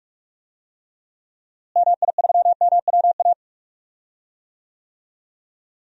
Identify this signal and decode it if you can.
Morse code, heard as a keyed tone, about 46 wpm, 700 Hz. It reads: MI3MWA
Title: DXCC-100 Call Signs - 6 Characters - Repeated 3x - 45wpm